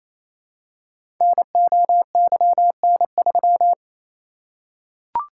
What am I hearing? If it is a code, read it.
Morse code, heard as a keyed tone, about 28 wpm, 700 Hz. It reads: NOYN3
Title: Code Groups - 5 Letter:Numbers - 28wpm